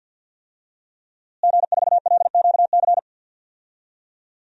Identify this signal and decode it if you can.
Morse code, heard as a keyed tone, about 46 wpm, 700 Hz. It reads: G4LQC
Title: DXCC-100 Call Signs - 5 Characters - Repeated 2x - 50wpm